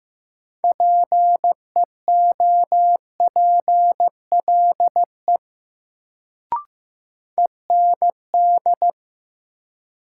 Morse code, heard as a keyed tone, about 15 wpm, 700 Hz. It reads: PEOPLE END
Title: Sentences from Top 200 Words - New Word Drill - 15wpm